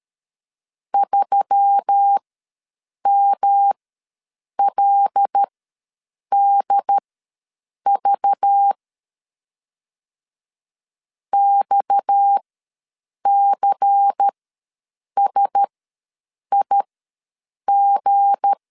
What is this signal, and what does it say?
Morse code, heard as a keyed tone, about 13 wpm, spaced as for 7 wpm, 800 Hz. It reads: 3MLDV XCSIG
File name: LEZIONI / 7 GF9Z / 7 G ws 12wpm 3140